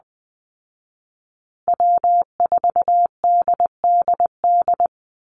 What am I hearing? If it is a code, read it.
Morse code, heard as a keyed tone, about 20 wpm, 700 Hz. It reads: W4DDD
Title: Call Signs Repeated 3x - 20wpm